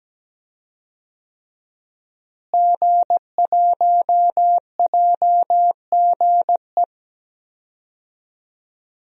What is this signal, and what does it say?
Morse code, heard as a keyed tone, about 17 wpm, 700 Hz. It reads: G1JGE
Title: DXCC-100 Call Signs - 5 Characters - Repeated 3x - 17wpm